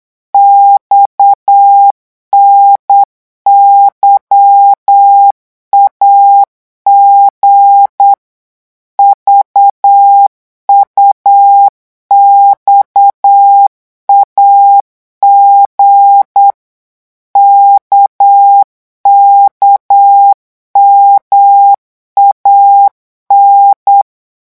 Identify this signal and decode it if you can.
Morse code, heard as a keyed tone, about 8 wpm, 800 Hz. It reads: XNYAG VUXAG KKMAN
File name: morse mp3 / revision3-vitesse 8 mots